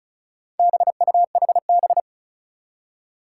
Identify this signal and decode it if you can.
Morse code, heard as a keyed tone, about 36 wpm, 700 Hz. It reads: BUHB
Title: Four Challenging Letter-Number Combinations 35wpm